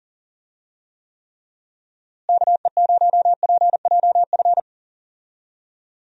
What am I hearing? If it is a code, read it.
Morse code, heard as a keyed tone, about 39 wpm, 700 Hz. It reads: KE0PJF
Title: USA Call Signs - 6 Characters - Repeated 3x - 40wpm